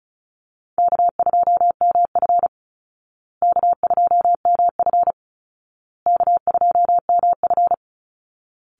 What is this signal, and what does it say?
Morse code, heard as a keyed tone, about 34 wpm, 700 Hz. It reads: K2MF K2MF K2MF
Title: Easy Call Signs Repeated 3x - 35wpm